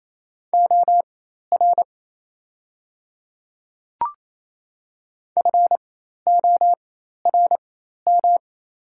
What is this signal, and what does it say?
Morse code, heard as a keyed tone, about 27 wpm, spaced as for 13 wpm, 700 Hz. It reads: OR FORM
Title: 4x Character Spacing - Top 200 Words - 28wpm